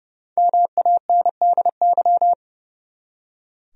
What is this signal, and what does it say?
Morse code, heard as a keyed tone, about 32 wpm, 700 Hz. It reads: MANDY
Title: USA Common Names - 30wpm